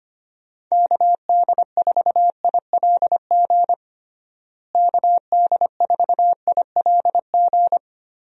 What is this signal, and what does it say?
Morse code, heard as a keyed tone, about 25 wpm, 700 Hz. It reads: KD4ILG KD4ILG
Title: USA Call Signs - 6 Characters - Repeated 2x - 25wpm